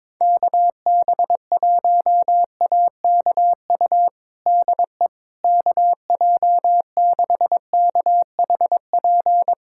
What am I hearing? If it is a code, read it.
Morse code, heard as a keyed tone, about 22 wpm, 700 Hz. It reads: KB1AKU DE KJ6KHP
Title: USA Callsign DE Callsign - 6 Characters - 22wpm